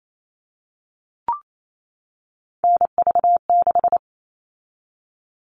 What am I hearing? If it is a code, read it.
Morse code, heard as a keyed tone, about 28 wpm, 700 Hz. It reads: NV6
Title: All Cut Numbers - 3-Digit 28wpm